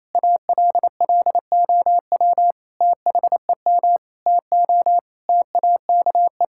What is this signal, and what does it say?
Morse code, heard as a keyed tone, about 28 wpm, 700 Hz. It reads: ALLOW THEM TO TAKE
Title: Sentences from Top 1000 Words - Review Part 5 - 28wpm